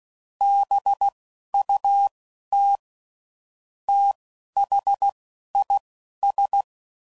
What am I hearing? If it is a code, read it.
Morse code, heard as a keyed tone, about 16 wpm, spaced as for 11 wpm, 800 Hz. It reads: BUT THIS
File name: CW Code Groups @10WPM / Weather Sets / Weather 3 17